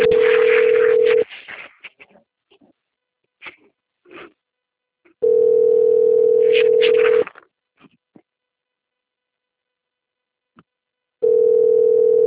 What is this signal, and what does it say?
Ringback tone (North America), 440 + 480 Hz.